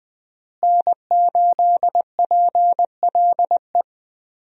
Morse code, heard as a keyed tone, about 21 wpm, 700 Hz. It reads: N8PLE
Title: USA Call Signs - 5 Characters - Repeated 3x - 20wpm